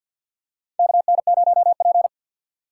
Morse code, heard as a keyed tone, about 49 wpm, 700 Hz. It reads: KN0P
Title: USA Call Signs - 4 Characters - 6x Character Spacing - 50wpm